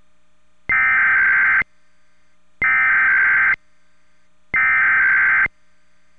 Emergency Alert System (SAME) tones (header bursts), heard as harsh buzzing bursts. It is heard from 0.7 to 5.4 s.